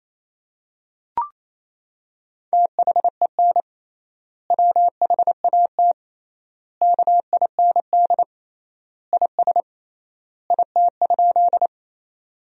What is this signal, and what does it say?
Morse code, heard as a keyed tone, about 28 wpm, 700 Hz. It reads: THEN WHAT KIND IS IT?
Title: 3x Word-Spacing - Sentences from Top 200 Words - 28wpm